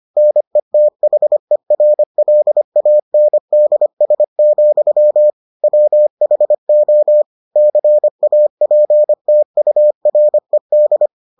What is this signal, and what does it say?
Morse code, heard as a keyed tone, about 25 wpm, 600 Hz. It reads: NETHERLANDS, WHO CAPTURED